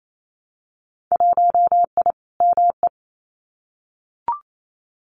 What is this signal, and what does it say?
Morse code, heard as a keyed tone, about 28 wpm, 700 Hz. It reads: 1I ME
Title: Winter Field Day Exchanges Encore 28wpm